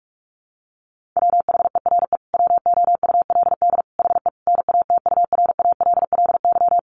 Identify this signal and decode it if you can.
Morse code, heard as a keyed tone, about 44 wpm, 700 Hz. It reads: WHERE WOULD HE NATURALLY
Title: Sentences from Top 1300 Words 45wpm